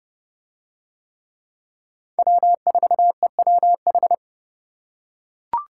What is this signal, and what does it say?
Morse code, heard as a keyed tone, about 27 wpm, 700 Hz. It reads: W4EWH